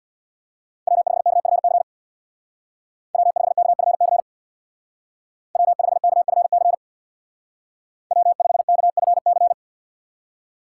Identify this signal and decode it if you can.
Morse code, heard as a keyed tone, about 64 wpm, 700 Hz. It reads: W5KFC W5KFC W5KFC W5KFC